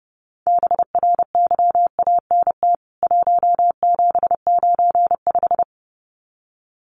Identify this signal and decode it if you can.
Morse code, heard as a keyed tone, about 30 wpm, 700 Hz. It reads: BRYANT 1795
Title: CWT Exchange 30wpm